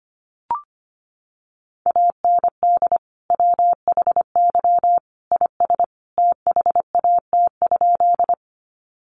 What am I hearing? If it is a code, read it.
Morse code, heard as a keyed tone, about 25 wpm, 700 Hz. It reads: AND WHY IS THAT?
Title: Sentences from Top 200 Words 25wpm